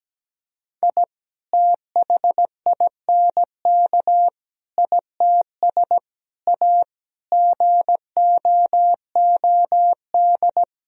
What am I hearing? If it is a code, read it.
Morse code, heard as a keyed tone, about 17 wpm, 700 Hz. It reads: I THINK ITS A GOOD
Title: Sentences from Top 1000 Words - Review Part 1 - 17wpm